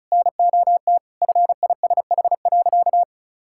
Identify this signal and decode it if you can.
Morse code, heard as a keyed tone, about 34 wpm, 700 Hz. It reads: NOT FISH.
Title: Sentences from Top 1600 Words 35wpm